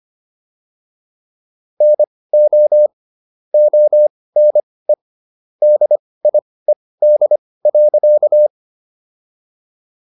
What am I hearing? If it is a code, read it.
Morse code, heard as a keyed tone, about 25 wpm, spaced as for 18 wpm, 600 Hz. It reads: NO ONE DIED.